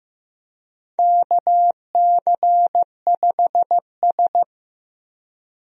Morse code, heard as a keyed tone, about 15 wpm, 700 Hz. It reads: KC5S
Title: USA Call Signs - 4 Characters - Repeated 2x - 15wpm